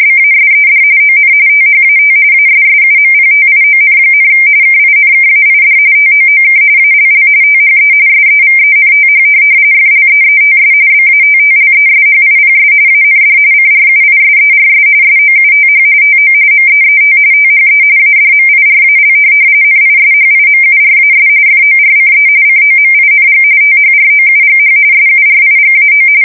RTTY, about 45.45 baud. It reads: BASED ON THAT ASSUMPTION, KINKAID TOOK ALL THE SEVENTH FLEETS BATTLESHIPS AND CRUISERS SOUTH TO MEET THE ADMIRAL NISHIMURAS FORCE COMING THROUGH SURIGAO
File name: fsk170 / USS-New-Jersey